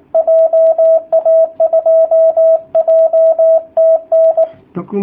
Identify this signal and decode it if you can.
Morse code, heard as a keyed tone, about 19 wpm, 660 Hz, with radio noise. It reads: JA2JTN
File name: jh8cbh / 2019cw / sp18- w